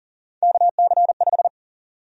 Morse code, heard as a keyed tone, about 40 wpm, 700 Hz. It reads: KC5
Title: USA Call Sign Prefixes - 40wpm